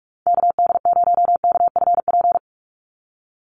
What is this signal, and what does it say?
Morse code, heard as a keyed tone, about 44 wpm, 700 Hz. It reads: KD0KFP
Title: USA Call Signs 45wpm